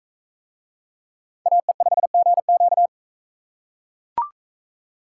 Morse code, heard as a keyed tone, about 42 wpm, 700 Hz. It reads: AE5GQ